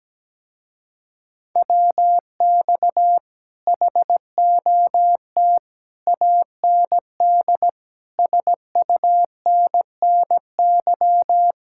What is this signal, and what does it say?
Morse code, heard as a keyed tone, about 17 wpm, 700 Hz. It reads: WX HOT AND SUNNY
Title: Speed-Racing Common QSO Phrases - 17wpm